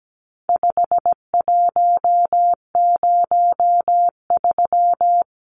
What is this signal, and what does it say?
Morse code, heard as a keyed tone, about 17 wpm, 700 Hz. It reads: 5103
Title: Numbers - 4-Digit 17wpm